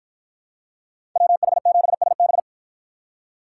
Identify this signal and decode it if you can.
Morse code, heard as a keyed tone, about 53 wpm, 700 Hz. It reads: WH7SB